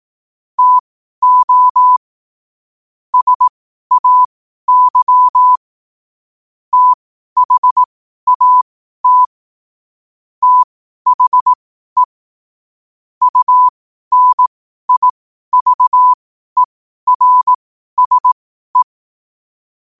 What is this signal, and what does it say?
Morse code, heard as a keyed tone, about 18 wpm, spaced as for 12 wpm, 1000 Hz. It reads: TO SAY THAT THE UNIVERSE
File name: qotdfiles / 12wpm